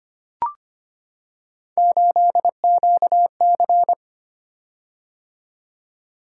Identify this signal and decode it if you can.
Morse code, heard as a keyed tone, about 25 wpm, 700 Hz. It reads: 8QC
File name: Three Letter-Number Combinations - 25wpm